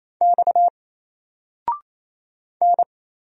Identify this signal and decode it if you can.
Morse code, heard as a keyed tone, about 29 wpm, 700 Hz. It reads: X N
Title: Single Letter-Number 28wpm